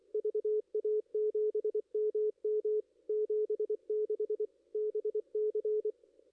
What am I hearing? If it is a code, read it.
Morse code, heard as a keyed tone, about 25 wpm, 440 Hz, with radio noise. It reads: VA7MM 76 BC
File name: contests / ss-cw-k5tr-op-n5rz -2300